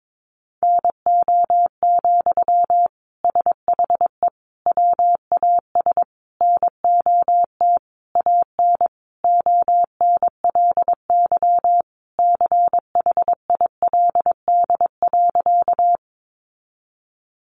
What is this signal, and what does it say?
Morse code, heard as a keyed tone, about 22 wpm, 700 Hz. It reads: NO, SHE WAS NOT AN ONLY CHILD.
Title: Sentences from Top 800 Words 22wpm